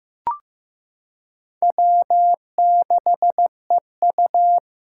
Morse code, heard as a keyed tone, about 15 wpm, 700 Hz. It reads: W6EU
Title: K1USN SST QSO Practice - 15wpm